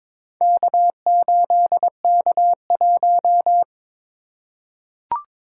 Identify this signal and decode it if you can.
Morse code, heard as a keyed tone, about 22 wpm, 700 Hz. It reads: K8K1
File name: practice / 22wpm / Four Letter-Number Combinations - 22wpm